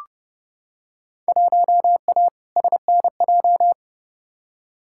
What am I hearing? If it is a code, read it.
Morse code, heard as a keyed tone, about 30 wpm, 700 Hz. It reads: 1A SNJ
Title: Field Day Exchanges - 30wpm